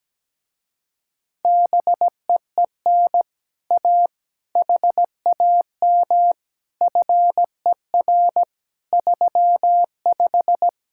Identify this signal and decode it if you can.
Morse code, heard as a keyed tone, about 17 wpm, 700 Hz. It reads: BEEN A HAM FER 35
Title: Speed-Racing Common QSO Phrases Encore - 17wpm